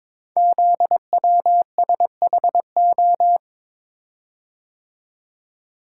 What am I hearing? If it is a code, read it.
Morse code, heard as a keyed tone, about 22 wpm, 700 Hz. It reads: ZWSHO
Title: Code Groups - 5 Letters - 22wpm